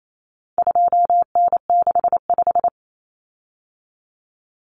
Morse code, heard as a keyed tone, about 28 wpm, 700 Hz. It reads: 2N65
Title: Common Cut Numbers - 4-Digit 28wpm